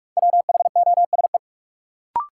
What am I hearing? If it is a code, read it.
Morse code, heard as a keyed tone, about 44 wpm, 700 Hz. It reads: WHOSE
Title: Sentences from Top 1200 Words - New Word Drill - 45wpm